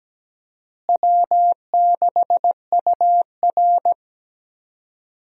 Morse code, heard as a keyed tone, about 17 wpm, 700 Hz. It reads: W6UR
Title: USA Call Signs - 4 Characters - Repeated 3x - 17wpm